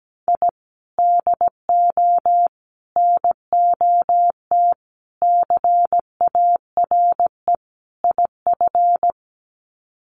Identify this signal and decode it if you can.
Morse code, heard as a keyed tone, about 17 wpm, 700 Hz. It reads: I DO NOT CARE IF
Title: Sets of 5 Words 17wpm